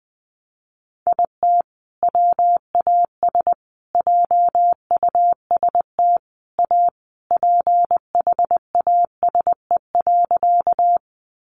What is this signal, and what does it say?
Morse code, heard as a keyed tone, about 20 wpm, 700 Hz. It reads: IT WAS JUST A PHASE.
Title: Sentences from Top 1200 Words 20wpm